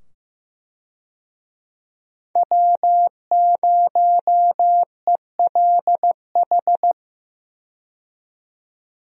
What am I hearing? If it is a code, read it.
Morse code, heard as a keyed tone, about 15 wpm, 700 Hz. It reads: W0ELH
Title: USA Call Signs - 5 Characters - 15wpm